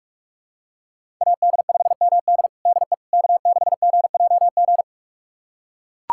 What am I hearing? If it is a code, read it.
Morse code, heard as a keyed tone, about 43 wpm, 700 Hz. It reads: AD5MD DE K6GJG
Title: USA Callsign DE Callsign - 5 Characters - 45wpm